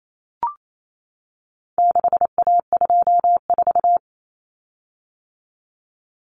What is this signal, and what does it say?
Morse code, heard as a keyed tone, about 28 wpm, 700 Hz. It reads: 6A24
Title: Common Cut Numbers - 4-Digit 28wpm